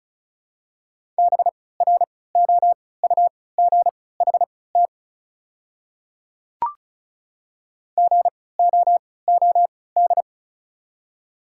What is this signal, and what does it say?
Morse code, heard as a keyed tone, about 36 wpm, spaced as for 20 wpm, 690 Hz. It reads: BROUGHT GOOD
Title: 3x Character Spacing - Top 500 Words - 35wpm